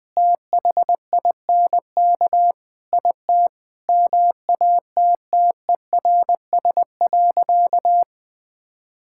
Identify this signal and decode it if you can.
Morse code, heard as a keyed tone, about 20 wpm, 700 Hz. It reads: THINK IT MATTERS.
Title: Sentences with 4-Word Phrases - 20wpm